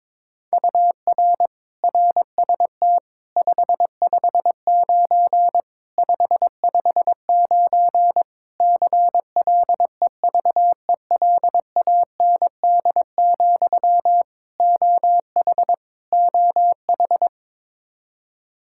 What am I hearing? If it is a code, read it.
Morse code, heard as a keyed tone, about 21 wpm, 700 Hz. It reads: UR RST 559 559 CLEVELAND, OH OH
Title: Practice QSO - Exchange 1 - Beginning - 22wpm